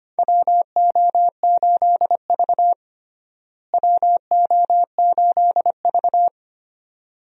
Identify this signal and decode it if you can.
Morse code, heard as a keyed tone, about 25 wpm, 700 Hz. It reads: WO8V WO8V
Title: USA Call Signs - 4 Characters - Repeated 2x - 25wpm